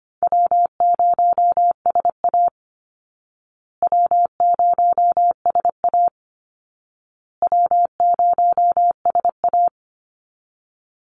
Morse code, heard as a keyed tone, about 25 wpm, 700 Hz. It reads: W0SA W0SA W0SA
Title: Call Signs Repeated 2x - 25wpm